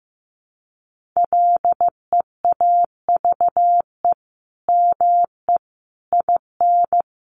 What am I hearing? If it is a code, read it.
Morse code, heard as a keyed tone, about 15 wpm, 700 Hz. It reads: LEAVE ME IN
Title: Sentences from Top 1500 Words 15wpm